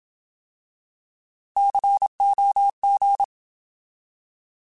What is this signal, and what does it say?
Morse code, heard as a keyed tone, about 27 wpm, 800 Hz. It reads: COG
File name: Alphabetical